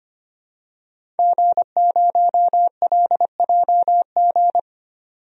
Morse code, heard as a keyed tone, about 25 wpm, 700 Hz. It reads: G0LJG
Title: DXCC-100 Call Signs - 5 Characters - 25wpm